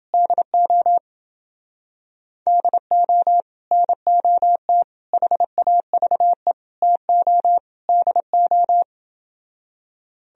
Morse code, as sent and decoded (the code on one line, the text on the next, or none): -.. --- / -.. --- / -. --- - / .... .- ...- . / - --- / -.. ---
DO DO NOT HAVE TO DO